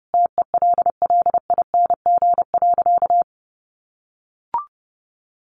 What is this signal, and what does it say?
Morse code, heard as a keyed tone, about 30 wpm, 700 Hz. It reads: TELLING.
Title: A Sound Of Thunder 30wpm